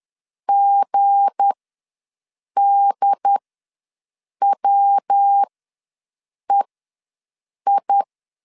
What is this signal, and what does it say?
Morse code, heard as a keyed tone, about 10 wpm, spaced as for 6 wpm, 800 Hz. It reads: GDWEI
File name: LEZIONI / 7 GF9Z / 7 G ws 10wpm 3843